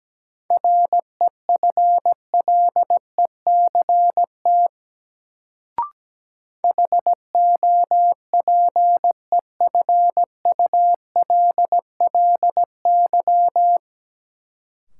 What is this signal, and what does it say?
Morse code, heard as a keyed tone, about 17 wpm, 700 Hz. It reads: REFLECT HOPEFULLY
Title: Sentences from Top 1500 Words - New Word Drill - 17wpm